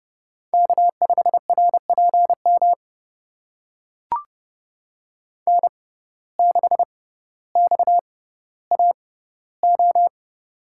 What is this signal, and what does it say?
Morse code, heard as a keyed tone, about 31 wpm, spaced as for 15 wpm, 700 Hz. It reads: K5RPM N6XAO
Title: USA Call Signs - 5 Characters - 6x Character Spacing - 30wpm